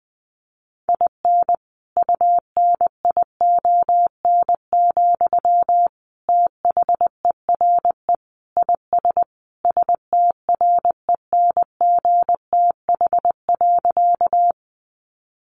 Morse code, heard as a keyed tone, about 20 wpm, 700 Hz. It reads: IN UNION, THERE IS STRENGTH.
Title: Sentences from Top 1400 Words 20wpm